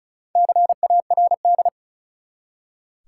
Morse code, heard as a keyed tone, about 34 wpm, 700 Hz. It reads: CARD
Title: Four Letter Words - 35wpm